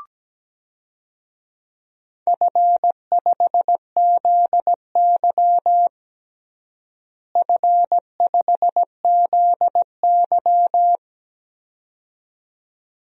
Morse code, heard as a keyed tone, about 17 wpm, 700 Hz. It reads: F5ZY F5ZY
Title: DXCC-100 Call Signs - 4 Characters - Repeated 2x - 17wpm